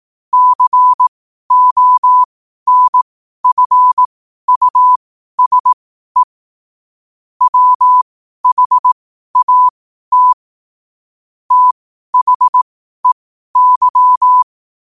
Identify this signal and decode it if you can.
Morse code, heard as a keyed tone, about 18 wpm, spaced as for 12 wpm, 1000 Hz. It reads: CONFUSE WHAT THEY
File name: qotdfiles / 12wpm